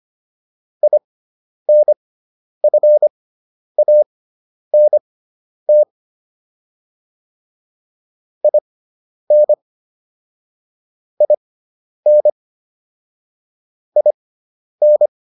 Morse code, heard as a keyed tone, about 25 wpm, spaced as for 10 wpm, 600 Hz. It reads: INFANT IN IN IN